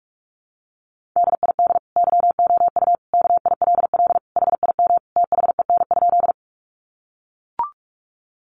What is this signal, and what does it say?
Morse code, heard as a keyed tone, about 44 wpm, 700 Hz. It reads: DID YOU KILL HIM THEN?
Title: Sentences from Top 1400 Words 45wpm